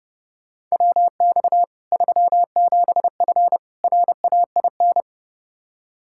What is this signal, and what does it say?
Morse code, heard as a keyed tone, about 30 wpm, 700 Hz. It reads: WX 37F RAIN
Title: Practice QSO - Exchange 2 - Info - 30wpm